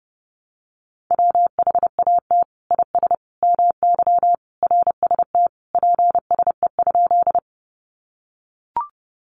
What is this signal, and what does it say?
Morse code, heard as a keyed tone, about 30 wpm, 700 Hz. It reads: WHAT IS MY RST PSE?
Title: Common QSO Phrases 30wpm